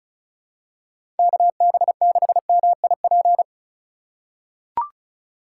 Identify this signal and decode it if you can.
Morse code, heard as a keyed tone, about 34 wpm, 700 Hz. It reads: KB6MIP
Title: USA Call Signs - 4x Character Spacing - 35wpm